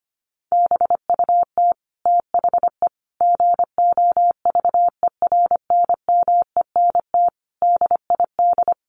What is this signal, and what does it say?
Morse code, heard as a keyed tone, about 25 wpm, 700 Hz. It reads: BUT THE GOVERNMENT DID
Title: Sentences from Top 1400 Words 25wpm